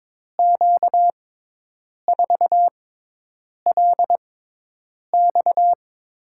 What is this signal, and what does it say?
Morse code, heard as a keyed tone, about 22 wpm, spaced as for 8 wpm, 700 Hz. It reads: Q4LX
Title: USA Call Signs - 5 Characters - 6x Character Spacing - 22wpm